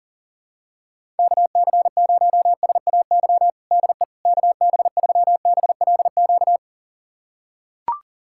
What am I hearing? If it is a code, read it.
Morse code, heard as a keyed tone, about 40 wpm, 700 Hz. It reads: KC0SAY DE KB3BLQ
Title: USA Callsign DE Callsign - 6 Characters - 40wpm